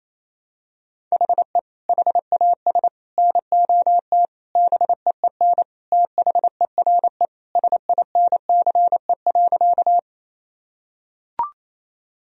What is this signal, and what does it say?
Morse code, heard as a keyed tone, about 28 wpm, 700 Hz. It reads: HE HAS NOT BEEN THERE SINCE.
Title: Sentences from Top 1000 Words - Review Part 1 - 28wpm